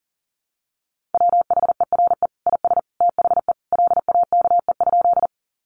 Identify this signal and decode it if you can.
Morse code, heard as a keyed tone, about 40 wpm, 700 Hz. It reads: WHERE IS THE LAKE?
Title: Sentences from Top 1100 Words 40wpm